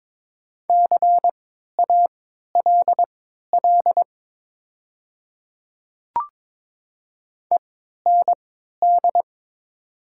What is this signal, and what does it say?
Morse code, heard as a keyed tone, about 21 wpm, spaced as for 12 wpm, 700 Hz. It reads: CALL END
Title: 3x Character Spacing - Top 200 Words - 22wpm